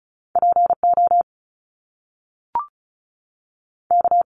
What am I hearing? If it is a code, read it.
Morse code, heard as a keyed tone, about 36 wpm, spaced as for 6 wpm, 700 Hz. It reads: POK
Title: Top Two Letter Combinations in Words 35wpm